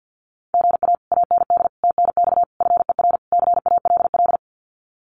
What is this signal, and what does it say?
Morse code, heard as a keyed tone, about 50 wpm, 700 Hz. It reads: GA AND TNX FER CALL
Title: Common QSO Phrases Encore 50wpm